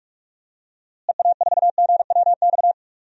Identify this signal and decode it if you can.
Morse code, heard as a keyed tone, about 46 wpm, 700 Hz. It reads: EA4GWX
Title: DXCC-100 Call Signs - 6 Characters - Repeated 2x - 45wpm